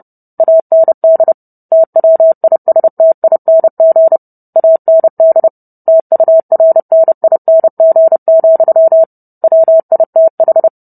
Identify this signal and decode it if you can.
Morse code, heard as a keyed tone, about 30 wpm, 650 Hz. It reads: AND TWISTING AND TURNING, WITH